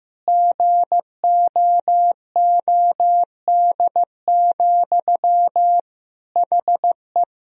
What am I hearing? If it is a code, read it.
Morse code, heard as a keyed tone, about 15 wpm, 700 Hz. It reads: GOOD, HE
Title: Sentences from Top 1000 Words - Review Part 1 - 15wpm